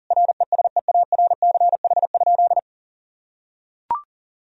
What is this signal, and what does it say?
Morse code, heard as a keyed tone, about 40 wpm, 700 Hz. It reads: RESEARCH?
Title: Sentences from Top 1000 Words - Review Part 4 - 40wpm